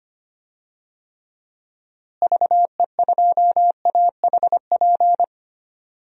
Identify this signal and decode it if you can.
Morse code, heard as a keyed tone, about 25 wpm, 700 Hz. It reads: VE2AHP
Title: DXCC-100 Call Signs - 6 Characters - Repeated 2x - 25wpm